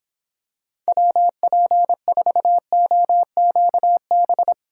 Morse code, heard as a keyed tone, about 26 wpm, 700 Hz. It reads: WP4OQB